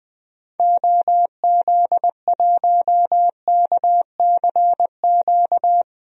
Morse code, heard as a keyed tone, about 20 wpm, 700 Hz. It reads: OZ1KCQ
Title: DXCC-100 Call Signs - 6 Characters - 20wpm